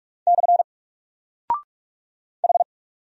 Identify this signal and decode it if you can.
Morse code, heard as a keyed tone, about 44 wpm, 700 Hz. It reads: / H
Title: Single Character (Letter:Number:Punctuation:Prosign) Alt 45wpm